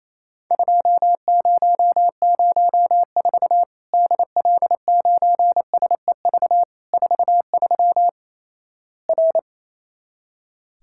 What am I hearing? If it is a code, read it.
Morse code, heard as a keyed tone, about 28 wpm, 700 Hz. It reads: 2004 DL9SEV 43 R
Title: Worked All Europe DX Contest - QTC Exchange - 28wpm